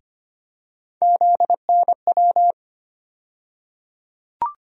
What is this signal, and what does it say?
Morse code, heard as a keyed tone, about 25 wpm, 700 Hz. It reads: ZNW
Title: USA Call Sign Postfixes - 3 Letters 25wpm